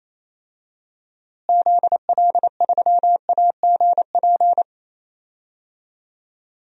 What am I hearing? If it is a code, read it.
Morse code, heard as a keyed tone, about 28 wpm, 700 Hz. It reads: ZL3AGP
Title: DXCC-100 Call Signs - 6 Characters - Repeated 2x - 28wpm